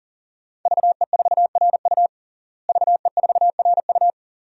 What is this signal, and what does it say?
Morse code, heard as a keyed tone, about 40 wpm, 700 Hz. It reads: VE4RU VE4RU
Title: DXCC-100 Call Signs - 5 Characters - Repeated 2x - 40wpm